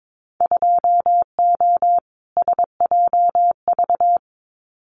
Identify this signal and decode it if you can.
Morse code, heard as a keyed tone, about 22 wpm, 700 Hz. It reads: 2O SJV
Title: Winter Field Day Exchanges Encore 22wpm